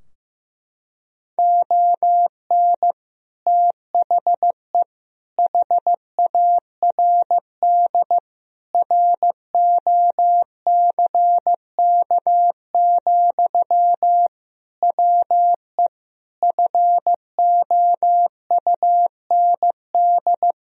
Morse code, heard as a keyed tone, about 15 wpm, 700 Hz. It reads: ON THE HARD ROCK, WE FOUND
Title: Sentences from Top 1000 Words - Review Part 2 - 15wpm